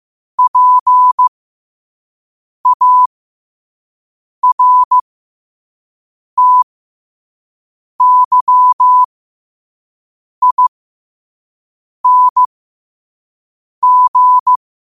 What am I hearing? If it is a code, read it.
Morse code, heard as a keyed tone, about 15 wpm, spaced as for 5 wpm, 1000 Hz. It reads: PARTYING
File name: qotdfiles / 5wpm